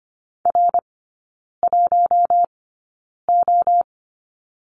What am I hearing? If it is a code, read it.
Morse code, heard as a keyed tone, about 25 wpm, spaced as for 9 wpm, 690 Hz. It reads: R1O
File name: Lesson 15e - Number 5 - Three Characters 25wpm